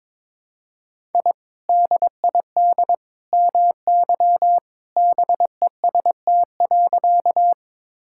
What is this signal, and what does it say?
Morse code, heard as a keyed tone, about 22 wpm, 700 Hz. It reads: I DID MY BEST.
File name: practice / 22wpm / Sentences with 2-Word Phrases - 22wpm